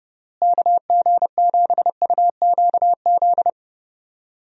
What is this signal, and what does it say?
Morse code, heard as a keyed tone, about 29 wpm, 700 Hz. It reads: KG7UQZ
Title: USA Call Signs - 6 Characters - Repeated 2x - 30wpm